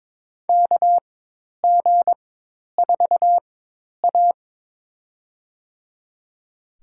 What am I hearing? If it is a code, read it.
Morse code, heard as a keyed tone, about 23 wpm, spaced as for 10 wpm, 700 Hz. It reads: KG4A
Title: USA Call Signs - 4 Characters - 4x Character Spacing - 22wpm